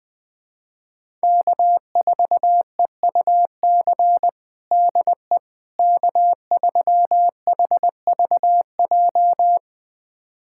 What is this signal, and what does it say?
Morse code, heard as a keyed tone, about 21 wpm, 700 Hz. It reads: K4EUC DE K3HVJ
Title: USA Callsign DE Callsign - 5 Characters - 20wpm